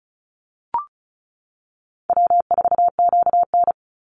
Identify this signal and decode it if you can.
Morse code, heard as a keyed tone, about 34 wpm, 700 Hz. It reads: W4QN
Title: Call Signs - 35wpm